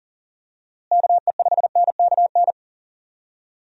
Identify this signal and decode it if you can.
Morse code, heard as a keyed tone, about 40 wpm, 700 Hz. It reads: KE5NKN
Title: USA Call Signs - 6 Characters - Repeated 3x - 40wpm